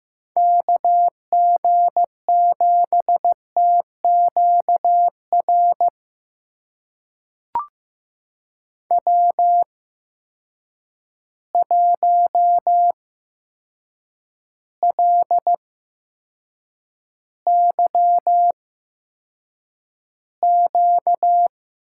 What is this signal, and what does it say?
Morse code, heard as a keyed tone, about 15 wpm, spaced as for 7 wpm, 700 Hz. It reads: KG7TQR W1LYQ
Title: USA Call Signs - 8x Character Spacing - 15wpm